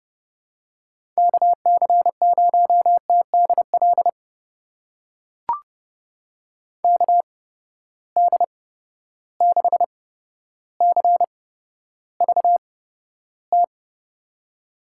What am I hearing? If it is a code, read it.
Morse code, heard as a keyed tone, about 29 wpm, spaced as for 13 wpm, 700 Hz. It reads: KC0TDL KD6CVT
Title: USA Call Signs - 8x Character Spacing - 30wpm